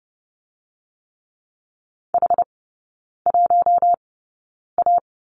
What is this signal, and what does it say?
Morse code, heard as a keyed tone, about 30 wpm, spaced as for 9 wpm, 700 Hz. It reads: H1A
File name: course / 30:10wpm / 087 Lesson 15e - Number 5 - Three Characters 30wpm